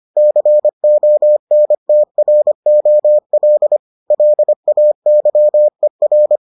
Morse code, heard as a keyed tone, about 25 wpm, 600 Hz. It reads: CONTROL LAYER